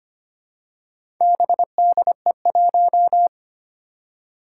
Morse code, heard as a keyed tone, about 25 wpm, 700 Hz. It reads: BDE1
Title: Four Challenging Letter-Number Combinations 25wpm